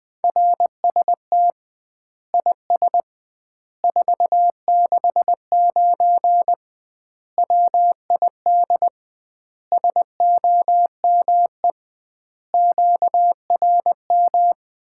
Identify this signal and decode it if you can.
Morse code, heard as a keyed tone, about 20 wpm, 700 Hz. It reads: RST IS 469 WID SOME QRM
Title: Common QSO Phrases - 2x Word-Spacing - 20wpm